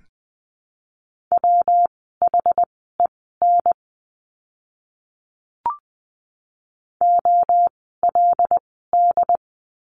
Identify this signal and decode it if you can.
Morse code, heard as a keyed tone, about 20 wpm, spaced as for 14 wpm, 700 Hz. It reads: WHEN OLD
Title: 192 Lesson 31f - Letter v - Words 20wpm